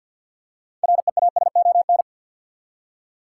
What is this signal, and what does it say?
Morse code, heard as a keyed tone, about 53 wpm, 700 Hz. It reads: REASON
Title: Six Letter Words - 50wpm